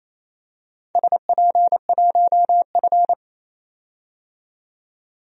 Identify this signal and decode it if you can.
Morse code, heard as a keyed tone, about 27 wpm, 700 Hz. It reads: SP1F
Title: DXCC-100 Call Signs - 4 Characters - 28wpm